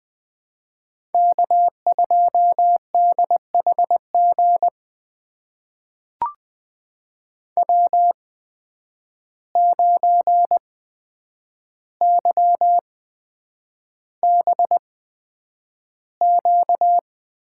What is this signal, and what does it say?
Morse code, heard as a keyed tone, about 20 wpm, spaced as for 9 wpm, 700 Hz. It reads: K2DHG W9YBQ
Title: USA Call Signs - 5 Characters - 8x Character Spacing - 20wpm